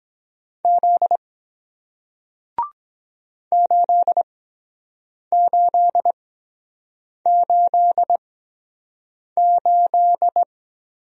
Morse code, heard as a keyed tone, about 23 wpm, spaced as for 7 wpm, 700 Hz. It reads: Z 8888